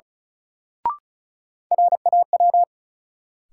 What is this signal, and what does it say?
Morse code, heard as a keyed tone, about 36 wpm, 700 Hz. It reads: RAW
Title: Sentences from Top 1500 Words - New Word Drill - 35wpm